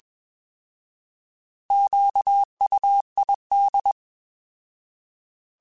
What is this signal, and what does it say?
Morse code, heard as a keyed tone, about 21 wpm, 800 Hz. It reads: QUID
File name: Alphabetical / Alphabetical mp3s / Qs @20